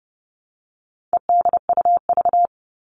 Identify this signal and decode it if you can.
Morse code, heard as a keyed tone, about 30 wpm, 700 Hz. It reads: EDUV
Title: All Cut Numbers - 4-Digit 30wpm